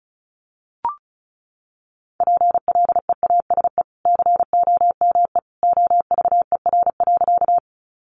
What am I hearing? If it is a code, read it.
Morse code, heard as a keyed tone, about 34 wpm, 700 Hz. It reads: PLEASE COME OVER.
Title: Sentences from Top 1500 Words 35wpm